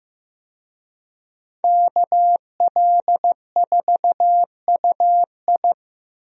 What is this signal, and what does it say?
Morse code, heard as a keyed tone, about 16 wpm, 700 Hz. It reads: KL4UI
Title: USA Call Signs - 5 Characters - Repeated 3x - 15wpm